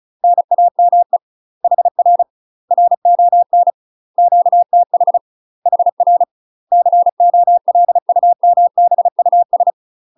Morse code, heard as a keyed tone, about 35 wpm, 700 Hz. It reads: NAME HR RON QTH HR COLUMBUS
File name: UK MASTER QSOs TRAINING / 35WPM / UK MASTER QSOs 0015